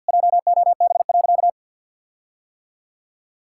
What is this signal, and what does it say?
Morse code, heard as a keyed tone, about 51 wpm, 700 Hz. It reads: JOB.
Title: Sentences from Top 1600 Words 50wpm